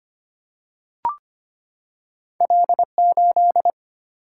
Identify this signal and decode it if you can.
Morse code, heard as a keyed tone, about 25 wpm, 700 Hz. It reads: L8
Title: Single-Letter Single-Number 25wpm